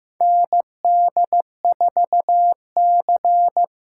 Morse code, heard as a keyed tone, about 15 wpm, 700 Hz. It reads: ND4C
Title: USA Call Signs - 4 Characters - 2x Character Spacing - 15wpm